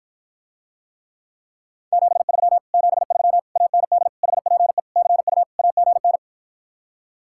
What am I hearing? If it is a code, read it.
Morse code, heard as a keyed tone, about 53 wpm, 700 Hz. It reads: 73 73 AND HPE CU AGN